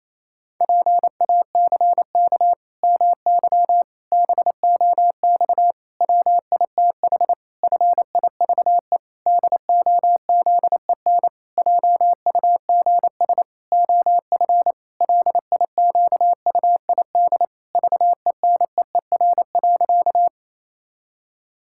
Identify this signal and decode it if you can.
Morse code, heard as a keyed tone, about 28 wpm, 700 Hz. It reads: PACK MY BOX WITH FIVE DOZEN JUGS OF LIQUID VENEER.
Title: Pangram Challenge v2 - 28wpm